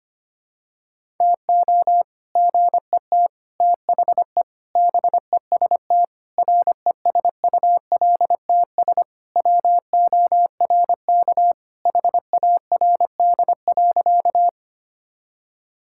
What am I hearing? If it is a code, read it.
Morse code, heard as a keyed tone, about 25 wpm, 700 Hz. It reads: TO GET THE BEST RESULTS WORK HARD.
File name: practice / 25wpm / Sentences with 3-Word Phrases - 25wpm